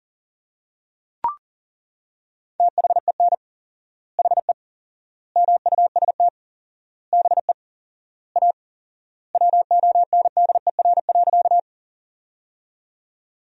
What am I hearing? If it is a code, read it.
Morse code, heard as a keyed tone, about 38 wpm, 700 Hz. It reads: THEN HE MUST BE A WONDER.
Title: 4x Word-Spacing - Sentences from Top 500 Words - 40wpm